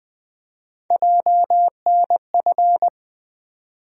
Morse code, heard as a keyed tone, about 20 wpm, 700 Hz. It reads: JNF
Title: USA Call Sign Postfixes - 3 Letters 20wpm